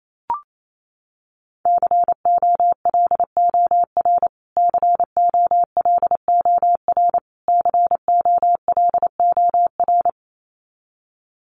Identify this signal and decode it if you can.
Morse code, heard as a keyed tone, about 28 wpm, 700 Hz. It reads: COLOR COLOR COLOR
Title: Top 500 Words Repeated 28wpm